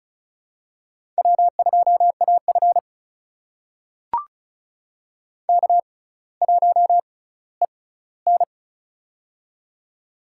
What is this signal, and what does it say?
Morse code, heard as a keyed tone, about 36 wpm, spaced as for 18 wpm, 700 Hz. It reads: W2AF K1EN
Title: USA Call Signs - 4 Characters - 6x Character Spacing - 35wpm